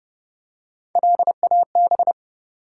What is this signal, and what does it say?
Morse code, heard as a keyed tone, about 30 wpm, 700 Hz. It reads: LAB
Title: Three Letter Words - 30wpm